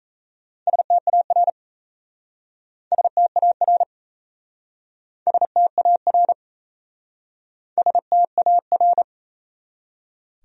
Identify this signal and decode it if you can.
Morse code, heard as a keyed tone, about 37 wpm, 700 Hz. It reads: STAR STAR STAR STAR